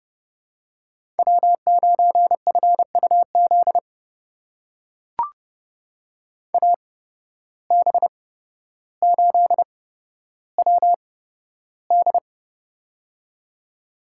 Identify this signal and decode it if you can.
Morse code, heard as a keyed tone, about 30 wpm, spaced as for 13 wpm, 700 Hz. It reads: W9FUZ AB8WD
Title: USA Call Signs - 5 Characters - 8x Character Spacing - 30wpm